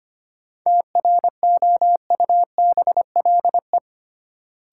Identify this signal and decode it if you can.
Morse code, heard as a keyed tone, about 25 wpm, 700 Hz. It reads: TROUBLE
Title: Sentences from Top 1200 Words - New Word Drill Alt - 25wpm